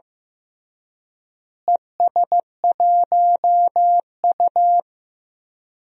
Morse code, heard as a keyed tone, about 15 wpm, 700 Hz. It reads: ES1U
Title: DXCC-100 Call Signs - 4 Characters - Repeated 3x - 15wpm